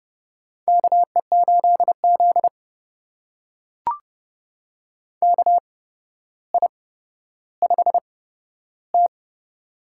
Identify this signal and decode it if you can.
Morse code, heard as a keyed tone, about 30 wpm, spaced as for 13 wpm, 700 Hz. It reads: KE8Z KI5T
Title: USA Call Signs - 8x Character Spacing - 30wpm